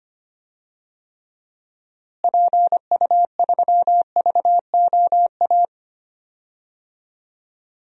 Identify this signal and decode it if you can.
Morse code, heard as a keyed tone, about 25 wpm, 700 Hz. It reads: PU3VOA
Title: DXCC-100 Call Signs - 6 Characters - Repeated 3x - 25wpm